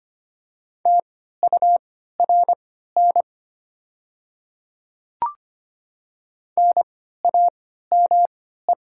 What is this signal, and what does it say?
Morse code, heard as a keyed tone, about 25 wpm, spaced as for 14 wpm, 690 Hz. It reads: TURN NAME
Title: 3x Character Spacing - Top 200 Words - 25wpm